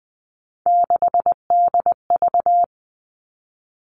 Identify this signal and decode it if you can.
Morse code, heard as a keyed tone, about 20 wpm, 700 Hz. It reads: 6DV
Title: All Cut Numbers - 3-Digit 20wpm